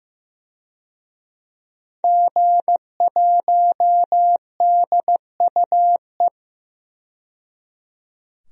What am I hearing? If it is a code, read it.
Morse code, heard as a keyed tone, about 15 wpm, 700 Hz. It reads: G1DUE
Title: DXCC-100 Call Signs - 5 Characters - 15wpm